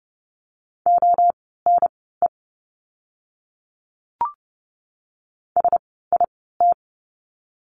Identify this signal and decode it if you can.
Morse code, heard as a keyed tone, about 30 wpm, spaced as for 17 wpm, 700 Hz. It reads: ONE SIT
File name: Lesson 16f - Letter c - Words 30wpm